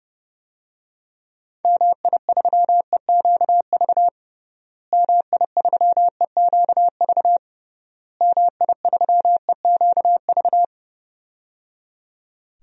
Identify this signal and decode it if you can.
Morse code, heard as a keyed tone, about 30 wpm, 700 Hz. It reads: MI3EQV MI3EQV MI3EQV
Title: DXCC-100 Call Signs - 6 Characters - Repeated 3x - 30wpm